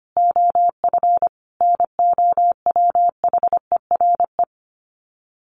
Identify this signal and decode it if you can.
Morse code, heard as a keyed tone, about 25 wpm, 700 Hz. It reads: OF NOWHERE
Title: Sets of 5 Words Encore 25wpm